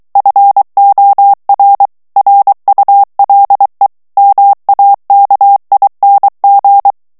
Morse code, heard as a keyed tone, about 23 wpm, 800 Hz. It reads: FOR RULE MAKING
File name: ANGLAIS texte clair 3 vitesse 22 mots minute